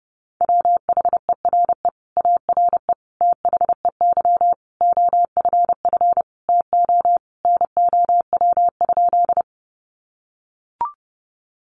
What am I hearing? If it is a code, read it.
Morse code, heard as a keyed tone, about 30 wpm, 700 Hz. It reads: WHERE ARE THEY OFF TO NOW?
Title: Sentences from Top 200 Words 30wpm